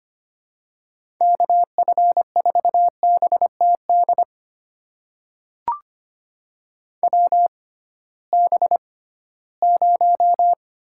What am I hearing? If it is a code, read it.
Morse code, heard as a keyed tone, about 25 wpm, spaced as for 16 wpm, 700 Hz. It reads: KF4BTD WB0
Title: USA Call Signs - 6 Characters - 6x Character Spacing - 25wpm